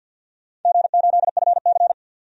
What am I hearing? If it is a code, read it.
Morse code, heard as a keyed tone, about 49 wpm, 700 Hz. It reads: G8FC
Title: DXCC-100 Call Signs - 4 Characters - 50wpm